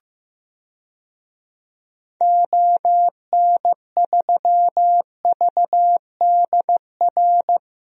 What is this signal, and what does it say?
Morse code, heard as a keyed tone, about 15 wpm, 700 Hz. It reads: ON3VDR
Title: DXCC-100 Call Signs - 6 Characters - Repeated 2x - 15wpm